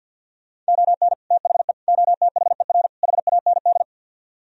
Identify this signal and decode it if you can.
Morse code, heard as a keyed tone, about 64 wpm, 700 Hz. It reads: ON THE OTHER HAND